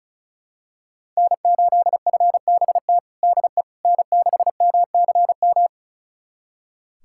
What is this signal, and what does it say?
Morse code, heard as a keyed tone, about 36 wpm, 700 Hz. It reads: N8FBT DE N6MCM
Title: USA Callsign DE Callsign - 5 Characters - 35wpm